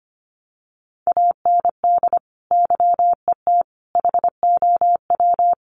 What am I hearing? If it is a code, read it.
Morse code, heard as a keyed tone, about 25 wpm, 700 Hz. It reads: AND YET HOW
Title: Sentences from Top 1300 Words 25wpm